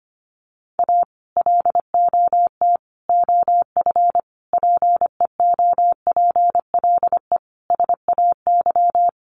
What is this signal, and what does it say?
Morse code, heard as a keyed tone, about 25 wpm, 700 Hz. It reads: A LOT OF PEOPLE SAY
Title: Sets of 5 Words Encore2 25wpm